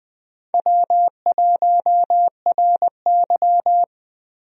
Morse code, heard as a keyed tone, about 21 wpm, 700 Hz. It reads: W1RY
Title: USA Call Signs - 4 Characters - Repeated 3x - 20wpm